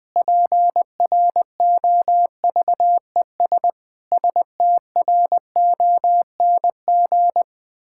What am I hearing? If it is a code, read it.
Morse code, heard as a keyed tone, about 20 wpm, 700 Hz. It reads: PROVES STRONG
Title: Pangram Challenge v2 - 20wpm